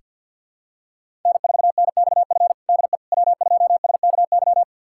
Morse code, heard as a keyed tone, about 46 wpm, 700 Hz. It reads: N4NXF DE W2SKY
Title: USA Callsign DE Callsign - 5 Characters - 50wpm